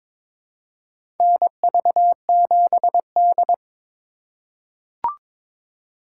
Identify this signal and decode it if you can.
Morse code, heard as a keyed tone, about 22 wpm, 700 Hz. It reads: NV7D
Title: USA Call Signs - 4 Characters - 3x Character Spacing - 22wpm